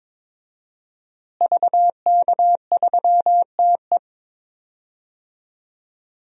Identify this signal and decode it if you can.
Morse code, heard as a keyed tone, about 22 wpm, 700 Hz. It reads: VK3TE
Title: DXCC-100 Call Signs - 5 Characters - Repeated 3x - 22wpm